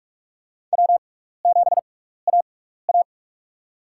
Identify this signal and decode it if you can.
Morse code, heard as a keyed tone, about 46 wpm, spaced as for 16 wpm, 700 Hz. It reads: W7AA
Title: USA Call Signs - 6x Character Spacing - 45wpm